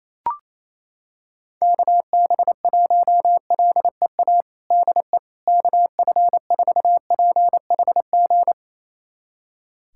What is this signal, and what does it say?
Morse code, heard as a keyed tone, about 28 wpm, 700 Hz. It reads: KB1LEA DE KF4PHG
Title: USA Callsign DE Callsign 28wpm